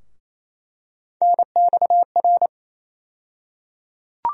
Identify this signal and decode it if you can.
Morse code, heard as a keyed tone, about 27 wpm, 700 Hz. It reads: NXR
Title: USA Call Sign Postfixes - 3 Letters 28wpm